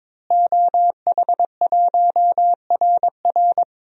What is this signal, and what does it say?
Morse code, heard as a keyed tone, about 22 wpm, 700 Hz. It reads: OH1RR
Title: DXCC-100 Call Signs - 5 Characters - Repeated 3x - 22wpm